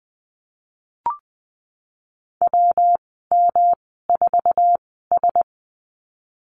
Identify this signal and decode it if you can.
Morse code, heard as a keyed tone, about 20 wpm, spaced as for 15 wpm, 700 Hz. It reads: WM4S
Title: 128 Lesson 21g - Number 6 - Call Signs 20wpm